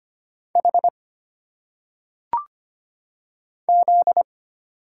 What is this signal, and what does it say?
Morse code, heard as a keyed tone, about 25 wpm, 700 Hz. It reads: H Z
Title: Single Letters - Phonetic Pronunciation 25wpm